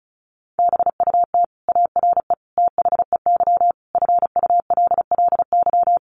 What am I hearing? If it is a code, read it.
Morse code, heard as a keyed tone, about 34 wpm, 700 Hz. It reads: BUT ARE THEY FULLY
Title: Sentences from Top 1300 Words 35wpm